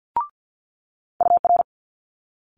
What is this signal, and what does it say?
Morse code, heard as a keyed tone, about 50 wpm, 700 Hz. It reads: UR
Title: Multi-letter Phonemes Quick Fire 50wpm